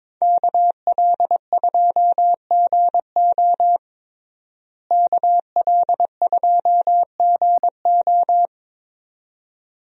Morse code, heard as a keyed tone, about 22 wpm, 700 Hz. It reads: KL2GO KL2GO
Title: USA Call Signs - 5 Characters - Repeated 2x - 22wpm